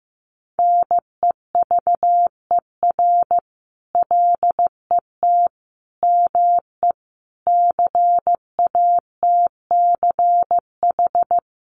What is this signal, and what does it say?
Morse code, heard as a keyed tone, about 15 wpm, 700 Hz. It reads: NEVER LET ME CATCH
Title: Sentences from Top 1500 Words 15wpm